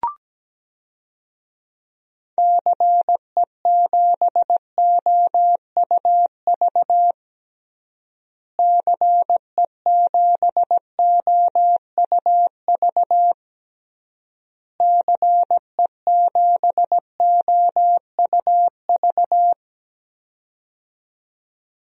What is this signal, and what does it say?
Morse code, heard as a keyed tone, about 17 wpm, 700 Hz. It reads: CE7OUV CE7OUV CE7OUV
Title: DXCC-100 Call Signs - 6 Characters - Repeated 3x - 17wpm